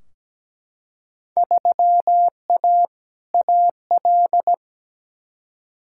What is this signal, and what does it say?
Morse code, heard as a keyed tone, about 17 wpm, 700 Hz. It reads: 3A AL
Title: Field Day Exchanges - 17wpm